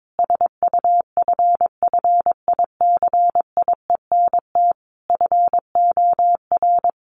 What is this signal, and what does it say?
Morse code, heard as a keyed tone, about 22 wpm, 700 Hz. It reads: SUFFICIENT FOR
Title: Sentences from Top 1400 Words 22wpm